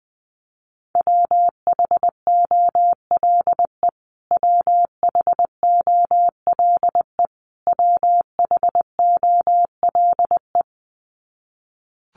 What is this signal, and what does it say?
Morse code, heard as a keyed tone, about 20 wpm, 700 Hz. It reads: WHOLE WHOLE WHOLE
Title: Top 500 Words Repeated 20wpm